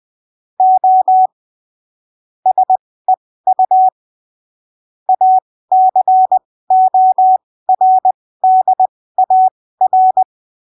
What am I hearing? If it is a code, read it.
Morse code, heard as a keyed tone, about 20 wpm, spaced as for 15 wpm, 750 Hz. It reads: O SEU ACORDAR
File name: arquivos / texto / texto 29 20ppm